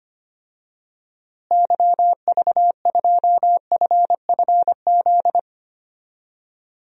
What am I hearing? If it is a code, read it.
Morse code, heard as a keyed tone, about 25 wpm, 700 Hz. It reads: YV2FFZ
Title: DXCC-100 Call Signs - 6 Characters - Repeated 3x - 25wpm